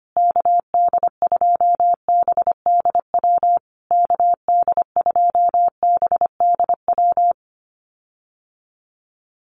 Morse code, heard as a keyed tone, about 25 wpm, 700 Hz. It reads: KD2BDW KD2BDW
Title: USA Call Signs Repeated 3x 25wpm